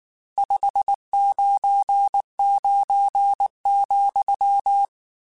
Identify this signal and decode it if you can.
Morse code, heard as a keyed tone, about 19 wpm, 800 Hz. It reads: 599,
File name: RSTs / RST-mp3s / Your RSTs @18